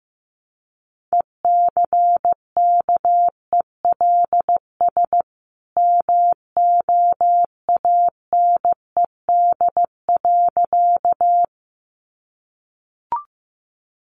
Morse code, heard as a keyed tone, about 15 wpm, 700 Hz. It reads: ECKELS MOANED.
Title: A Sound Of Thunder 15wpm